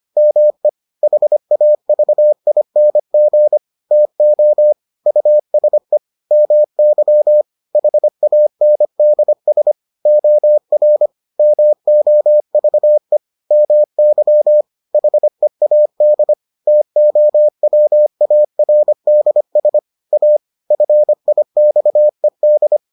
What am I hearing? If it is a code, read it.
Morse code, heard as a keyed tone, about 25 wpm, 600 Hz. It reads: ME HAVING TO USE MY HANDS OR MOVE MY HEAD TOWARDS A FIXED